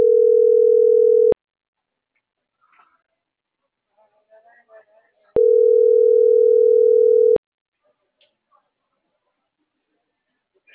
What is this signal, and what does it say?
Ringback tone (North America), 440 + 480 Hz.